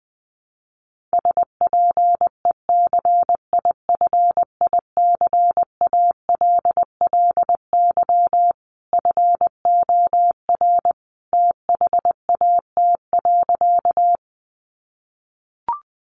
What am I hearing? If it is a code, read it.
Morse code, heard as a keyed tone, about 20 wpm, 700 Hz. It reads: SPECIFICALLY FOR THAT.
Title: Sentences from Top 1100 Words 20wpm